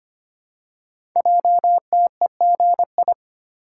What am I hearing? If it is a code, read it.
Morse code, heard as a keyed tone, about 25 wpm, 700 Hz. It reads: JTEGI
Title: Code Groups - 5 Letters - 25wpm